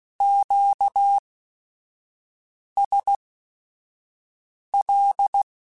Morse code, heard as a keyed tone, about 16 wpm, spaced as for 5 wpm, 800 Hz. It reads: QSL
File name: Q Calls / Q Calls-mp3s / Q Calls Once @5